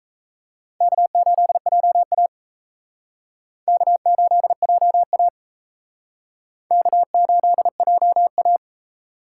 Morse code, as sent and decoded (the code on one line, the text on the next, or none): -.- ---.. .--- .- / -.- ---.. .--- .- / -.- ---.. .--- .-
K8JA K8JA K8JA